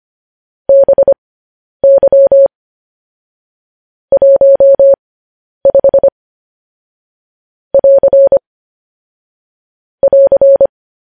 Morse code, heard as a keyed tone, about 25 wpm, spaced as for 10 wpm, 550 Hz. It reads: BY 15 <AR> <AR>
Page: news-headlines-in-morse-code-at-10-wpm